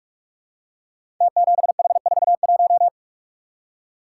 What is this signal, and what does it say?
Morse code, heard as a keyed tone, about 44 wpm, 700 Hz. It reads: T7H41
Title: Code Groups - 5 Letter:Numbers - 45wpm